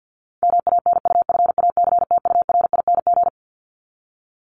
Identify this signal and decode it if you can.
Morse code, heard as a keyed tone, about 50 wpm, 700 Hz. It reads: MANUFACTURING
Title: ING Suffix - Top 100 Words with Base Words 50wpm